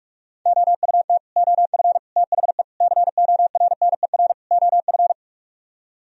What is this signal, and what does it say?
Morse code, heard as a keyed tone, about 44 wpm, 700 Hz. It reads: OUT OF THE CORNER OF